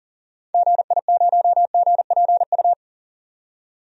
Morse code, heard as a keyed tone, about 40 wpm, 700 Hz. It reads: GI0GPU
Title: DXCC-100 Call Signs - 6 Characters - 40wpm